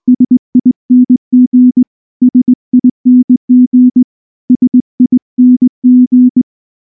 Morse code, heard as a keyed tone, about 18 wpm, 250 Hz. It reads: SING SING SING
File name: morse / morse-training-20WPM-1